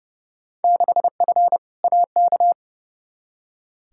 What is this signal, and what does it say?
Morse code, heard as a keyed tone, about 30 wpm, 700 Hz. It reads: 6F AK
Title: Field Day Exchanges Encore - 30wpm